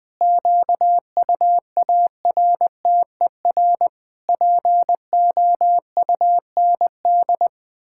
Morse code, heard as a keyed tone, about 21 wpm, 700 Hz. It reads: QUARTER POUND
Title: Pangram Challenge v2 - 20wpm